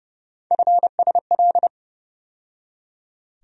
Morse code, heard as a keyed tone, about 30 wpm, 700 Hz. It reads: FSL
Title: USA Call Sign Postfixes - 3 Letters 30wpm